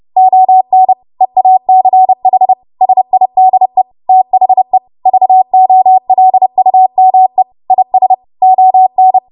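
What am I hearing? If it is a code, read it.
Morse code, heard as a keyed tone, about 30 wpm, 750 Hz. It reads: ON EACH SIDE THE VOLUME IS ON
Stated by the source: Morse practice transmission at 30 WPM.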